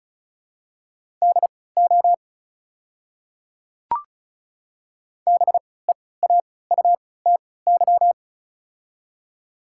Morse code, heard as a keyed tone, about 36 wpm, spaced as for 20 wpm, 690 Hz. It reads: DO BEAUTY
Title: 3x Character Spacing - Top 500 Words - 35wpm